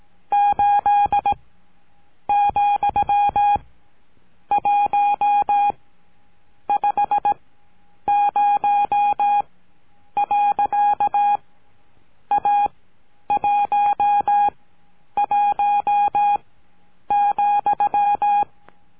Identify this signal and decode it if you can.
Morse code, heard as a keyed tone, about 18 wpm, spaced as for 8 wpm, 820 Hz. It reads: 8,150. A 11,
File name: cursos / CURSO DE TELEGRAFIA / Telegrafia - Trilha 19